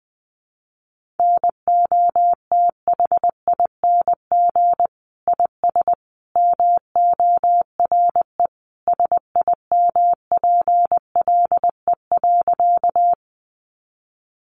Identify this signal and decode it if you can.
Morse code, heard as a keyed tone, about 20 wpm, 700 Hz. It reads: NOTHING IS MORE SIMPLE.
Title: Sentences from Top 500 Words 20wpm